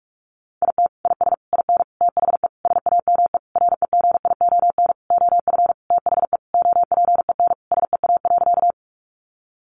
Morse code, heard as a keyed tone, about 46 wpm, 700 Hz. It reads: IT IS IN THE SAME REGION OF THE OPEN SEA.
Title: Sentences from Top 1500 Words 45wpm